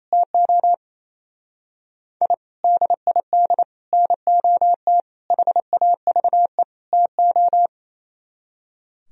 Morse code, heard as a keyed tone, about 36 wpm, 700 Hz. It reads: TO I DID NOT HAVE TO